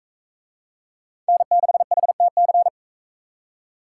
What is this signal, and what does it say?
Morse code, heard as a keyed tone, about 37 wpm, 700 Hz. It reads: N6HTC